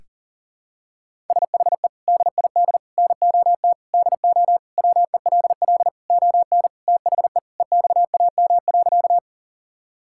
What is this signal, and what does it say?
Morse code, heard as a keyed tone, about 40 wpm, 700 Hz. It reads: SHE DID NOT DO WELL ON THE EXAM.
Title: Sentences from Top 1000 Words - Review Part 5 - 40wpm